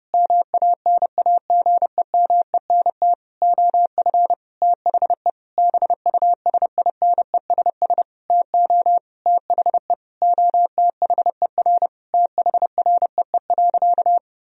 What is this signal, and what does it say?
Morse code, heard as a keyed tone, about 32 wpm, 700 Hz. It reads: MANAGEMENT OF THE BUSINESS TO THE OTHER THREE.
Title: Sentences from Top 1000 Words - Review Part 4 - 30wpm